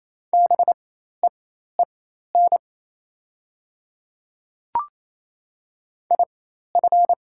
Morse code, heard as a keyed tone, about 27 wpm, spaced as for 13 wpm, 690 Hz. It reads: BEEN IF
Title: 4x Character Spacing - Top 100 Words - 28wpm